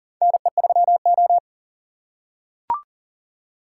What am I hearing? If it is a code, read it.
Morse code, heard as a keyed tone, about 40 wpm, 700 Hz. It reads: NE3O
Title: USA Call Signs - 4 Characters - 6x Character Spacing - 40wpm